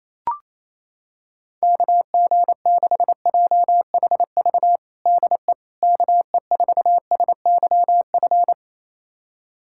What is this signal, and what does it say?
Morse code, heard as a keyed tone, about 28 wpm, 700 Hz. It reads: KG6JHV DE KE4SYF
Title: USA Callsign DE Callsign - 6 Characters - 28wpm